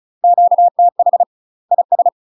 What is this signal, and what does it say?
Morse code, heard as a keyed tone, about 33 wpm, 700 Hz. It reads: QTH IS
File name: UK MASTER QSOs TRAINING / 35WPM / UK MASTER QSOs 0013